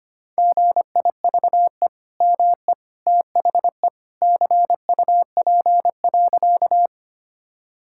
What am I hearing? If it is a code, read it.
Morse code, heard as a keyed tone, about 25 wpm, 700 Hz. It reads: GIVE ME THE CUP.
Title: Sentences from Top 1600 Words 25wpm